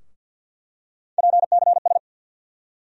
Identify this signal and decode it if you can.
Morse code, heard as a keyed tone, about 47 wpm, 700 Hz. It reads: PCS
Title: USA Call Sign Postfixes - 3 Letters 50wpm